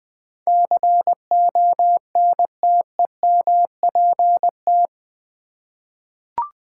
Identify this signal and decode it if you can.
Morse code, heard as a keyed tone, about 20 wpm, 700 Hz. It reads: CONTEMPT
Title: 8 Letter Words - First 3 Letters Given - 20wpm